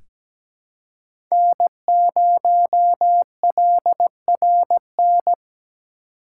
Morse code, heard as a keyed tone, about 17 wpm, 700 Hz. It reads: N0LRN
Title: USA Call Signs - 5 Characters - 2x Character Spacing - 17wpm